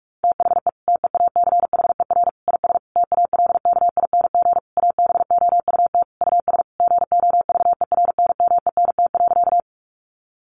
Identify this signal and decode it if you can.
Morse code, heard as a keyed tone, about 46 wpm, 700 Hz. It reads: THE TEACHER IS TALKING ABOUT US GOVERNMENT.
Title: Easy Phrases 45wpm